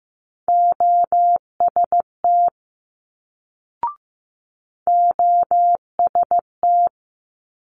Morse code, heard as a keyed tone, about 15 wpm, 700 Hz. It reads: OST OST
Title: Multi-letter Phonemes Quick Fire 15wpm